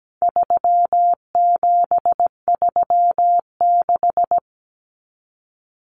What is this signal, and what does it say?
Morse code, heard as a keyed tone, about 17 wpm, 700 Hz. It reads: 3736
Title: Numbers - 4-Digit 17wpm